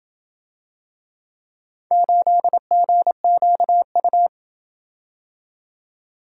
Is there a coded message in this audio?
Yes: Morse code from 1.9 to 4.3 s, heard as a keyed tone, about 28 wpm, 700 Hz. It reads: 8GQU